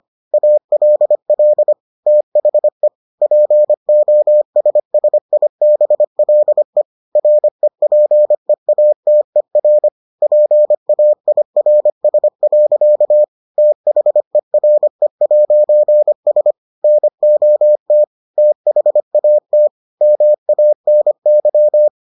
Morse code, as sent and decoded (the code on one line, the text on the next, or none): .- .-.. .-.. / - .... / .--. --- ... ... .. -... .-.. / .-. .--. .- - . .-. / .--. .- .. .-. ... .-.-.- / - .... .-. .----. ... / -. --- - / - .... .- - / -- .- -. -.--
ALL THE POSSIBLE REPEATER PAIRS. THERE'S NOT THAT MANY